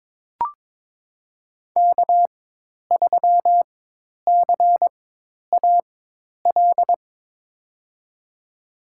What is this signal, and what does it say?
Morse code, heard as a keyed tone, about 23 wpm, spaced as for 10 wpm, 700 Hz. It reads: K3CAL
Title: USA Call Signs - 5 Characters - 4x Character Spacing - 22wpm